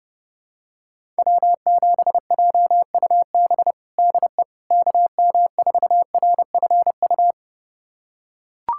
Morse code, heard as a keyed tone, about 32 wpm, 700 Hz. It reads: W7JUB DE KM4RFU
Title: USA Callsign DE Callsign 30wpm